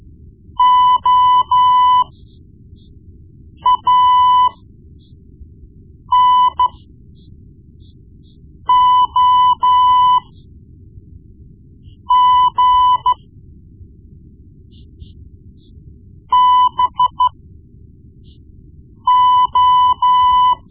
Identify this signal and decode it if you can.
Morse code, heard as a keyed tone, about 14 wpm, spaced as for 4 wpm, 1000 Hz. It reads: OANOG BO
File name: Morse Lessons / Morse Lessons / Tape-1B